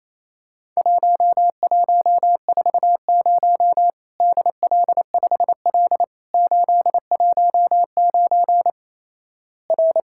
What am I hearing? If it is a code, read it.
Morse code, heard as a keyed tone, about 28 wpm, 700 Hz. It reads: 1140 DL5L 819 R
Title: Worked All Europe DX Contest - QTC Exchange - 28wpm